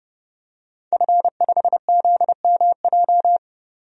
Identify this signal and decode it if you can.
Morse code, heard as a keyed tone, about 30 wpm, 700 Hz. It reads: F5ZMJ
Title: DXCC-100 Call Signs - 5 Characters - 30wpm